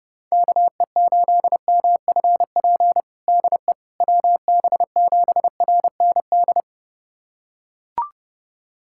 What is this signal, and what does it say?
Morse code, heard as a keyed tone, about 30 wpm, 700 Hz. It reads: KE8MFP DE WB7RND
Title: USA Callsign DE Callsign - 6 Characters - 30wpm